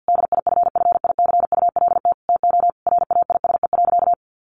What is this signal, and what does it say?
Morse code, heard as a keyed tone, about 50 wpm, 700 Hz. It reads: DIFFICULT TO RAISE.
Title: Sentences from Top 1400 Words 50wpm